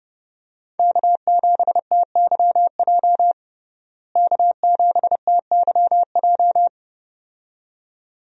Morse code, heard as a keyed tone, about 31 wpm, 700 Hz. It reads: K7TYJ K7TYJ
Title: USA Call Signs - 5 Characters - Repeated 2x - 30wpm